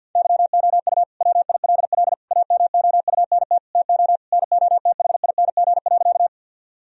Morse code, heard as a keyed tone, about 50 wpm, 700 Hz. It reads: YOU WILL AMOUNT TO NOTHING.
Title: Sentences from Top 1000 Words - Review Part 4 - 50wpm